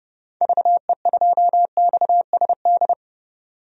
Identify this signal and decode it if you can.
Morse code, heard as a keyed tone, about 31 wpm, 700 Hz. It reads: VE2XSD
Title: DXCC-100 Call Signs - 6 Characters - 30wpm